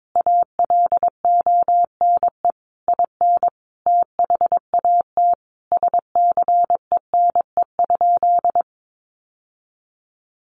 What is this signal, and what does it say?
Morse code, heard as a keyed tone, about 22 wpm, 700 Hz. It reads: ALONE IN THAT SCENE?
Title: Sentences from Top 1100 Words 22wpm